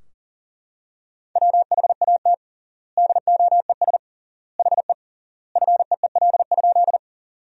Morse code, heard as a keyed tone, about 40 wpm, 700 Hz. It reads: WHAT DOES HE FEEL?
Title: 3x Word-Spacing - Sentences from Top 400 Words - 40wpm